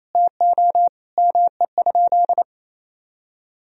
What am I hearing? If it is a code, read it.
Morse code, heard as a keyed tone, about 28 wpm, 700 Hz. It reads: TO ME?
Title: Sentences from Top 1000 Words - Review Part 3 - 28wpm